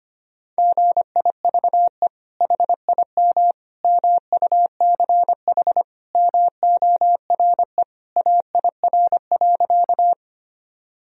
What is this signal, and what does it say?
Morse code, heard as a keyed tone, about 25 wpm, 700 Hz. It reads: GIVE HIM MUCH MORE AIR.
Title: Sentences from Top 1000 Words - Review Part 1 - 25wpm